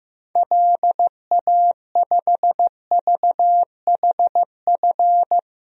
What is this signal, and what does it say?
Morse code, heard as a keyed tone, about 15 wpm, 700 Hz. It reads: LA5VHF
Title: DXCC-100 Call Signs - 6 Characters - Repeated 3x - 15wpm